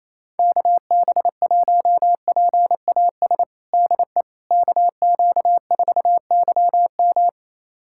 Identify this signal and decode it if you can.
Morse code, heard as a keyed tone, about 28 wpm, 700 Hz. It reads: KB1PAS DE KQ4YM
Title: USA Callsign DE Callsign 28wpm